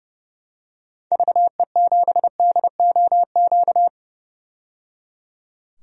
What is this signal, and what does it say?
Morse code, heard as a keyed tone, about 30 wpm, 700 Hz. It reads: VE7DOQ
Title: DXCC-100 Call Signs - 6 Characters - 30wpm